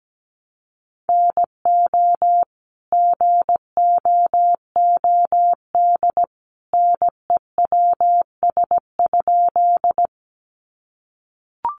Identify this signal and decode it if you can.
Morse code, heard as a keyed tone, about 17 wpm, 700 Hz. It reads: NO GOOD NEWS?
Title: Sentences from Top 800 Words 17wpm